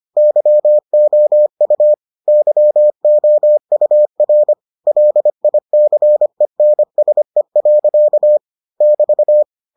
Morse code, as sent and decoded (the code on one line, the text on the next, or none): -.-- --- ..- / -.-- --- ..- .-. / .-.. .. -.-. -. ... .-.-.- / -...-
YOU YOUR LICENSE. <BT>